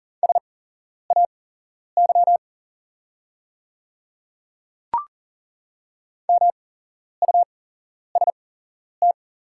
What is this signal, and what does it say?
Morse code, heard as a keyed tone, about 40 wpm, spaced as for 11 wpm, 690 Hz. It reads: SAY MUST
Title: 8x Character Spacing - Top 300 Words - 40wpm